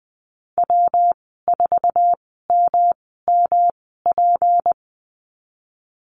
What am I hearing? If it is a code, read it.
Morse code, heard as a keyed tone, about 20 wpm, spaced as for 14 wpm, 700 Hz. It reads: W4MMP
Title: 238 Lesson 38g - Letter q - Call Signs 20wpm